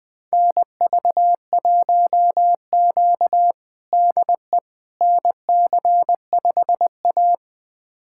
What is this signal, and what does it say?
Morse code, heard as a keyed tone, about 21 wpm, 700 Hz. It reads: NV1Q DE NC5A
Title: USA Callsign DE Callsign - 4 Characters - 20wpm